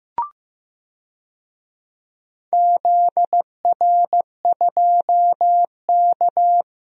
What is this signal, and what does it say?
Morse code, heard as a keyed tone, about 16 wpm, 700 Hz. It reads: ZR2K
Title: DXCC-100 Call Signs - 4 Characters - 15wpm